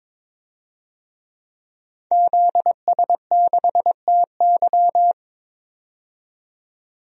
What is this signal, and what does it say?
Morse code, heard as a keyed tone, about 21 wpm, 700 Hz. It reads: ZS6TY
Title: DXCC-100 Call Signs - 5 Characters - 22wpm